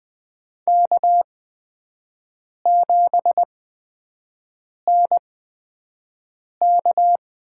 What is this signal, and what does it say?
Morse code, heard as a keyed tone, about 20 wpm, spaced as for 5 wpm, 700 Hz. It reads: K7NK
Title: USA Call Signs - 8x Character Spacing - 20wpm